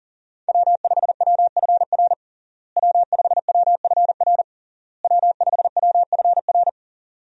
Morse code, heard as a keyed tone, about 37 wpm, 700 Hz. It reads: W5WFR W5WFR W5WFR
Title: USA Call Signs - 5 Characters - Repeated 3x - 40wpm